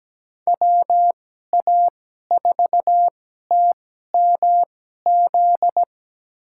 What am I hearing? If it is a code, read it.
Morse code, heard as a keyed tone, about 17 wpm, spaced as for 12 wpm, 700 Hz. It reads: WA4TMZ
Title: USA Call Signs - 6 Characters - 2x Character Spacing - 17wpm